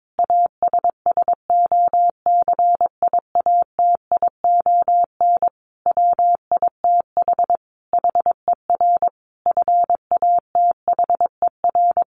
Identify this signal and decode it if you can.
Morse code, heard as a keyed tone, about 22 wpm, 700 Hz. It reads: ASSOCIATION WITH HER FATHER
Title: Sentences from Top 1100 Words 22wpm